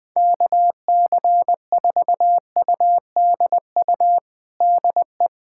Morse code, heard as a keyed tone, about 21 wpm, 700 Hz. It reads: KC4UDU DE
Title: USA Callsign DE Callsign - 6 Characters - 20wpm